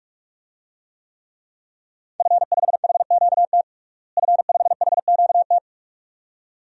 Morse code, heard as a keyed tone, about 46 wpm, 700 Hz. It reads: F5HQT F5HQT
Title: DXCC-100 Call Signs - 5 Characters - Repeated 2x - 45wpm